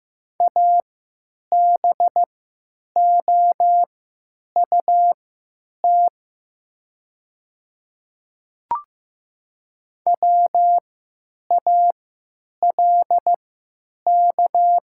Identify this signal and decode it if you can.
Morse code, heard as a keyed tone, about 15 wpm, spaced as for 9 wpm, 700 Hz. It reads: ABOUT WALK
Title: 3x Character Spacing - Top 300 Words - 15wpm